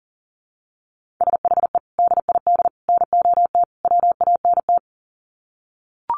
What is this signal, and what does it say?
Morse code, heard as a keyed tone, about 40 wpm, 700 Hz. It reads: SHE DID NOT WANT
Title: Sets of 4 Words 40wpm